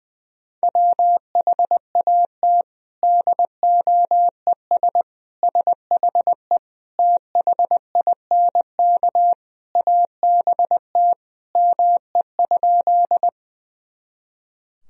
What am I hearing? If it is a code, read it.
Morse code, heard as a keyed tone, about 21 wpm, 700 Hz. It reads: WHAT DOES SHE THINK ABT ME?
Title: Sentences from Top 200 Words and Abbreviations 20wpm